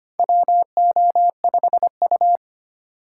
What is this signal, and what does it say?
Morse code, heard as a keyed tone, about 25 wpm, 700 Hz. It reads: WO5U
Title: USA Call Signs - 4 Characters - 2x Character Spacing - 25wpm